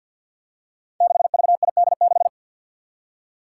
Morse code, heard as a keyed tone, about 52 wpm, 700 Hz. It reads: 6VID6
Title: Five Challenging Letter-Number Combinations 50wpm